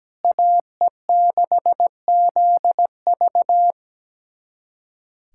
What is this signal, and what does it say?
Morse code, heard as a keyed tone, about 17 wpm, 700 Hz. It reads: AE6ZV
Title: USA Callsign DE Callsign - 5 Characters - 17wpm